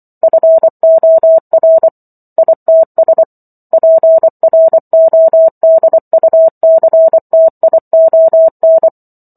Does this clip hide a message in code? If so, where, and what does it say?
Yes: Morse code from 0.2 to 8.9 s, heard as a keyed tone, about 24 wpm, 650 Hz. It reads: FOR ITS PRODUCTION